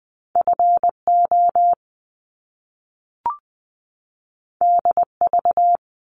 Morse code, heard as a keyed tone, about 20 wpm, 700 Hz. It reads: FO DV
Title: Top Two Letter Combinations in Words 20wpm